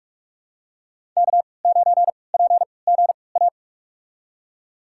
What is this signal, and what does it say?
Morse code, heard as a keyed tone, about 46 wpm, spaced as for 25 wpm, 700 Hz. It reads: K9PGA
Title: USA Call Signs - 3x Character Spacing - 45wpm